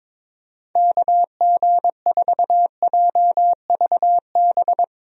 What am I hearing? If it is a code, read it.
Morse code, heard as a keyed tone, about 22 wpm, 700 Hz. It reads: KG4JVB
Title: USA Call Signs - 6 Characters - Repeated 2x - 22wpm